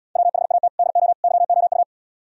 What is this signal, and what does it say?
Morse code, heard as a keyed tone, about 76 wpm, 700 Hz. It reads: WHAT DO YOU